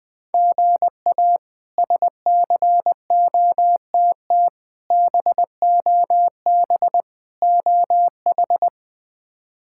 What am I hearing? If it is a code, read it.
Morse code, heard as a keyed tone, about 21 wpm, 700 Hz. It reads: GA SCOTT BOB OH
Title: K1USN SST QSO Practice - 20wpm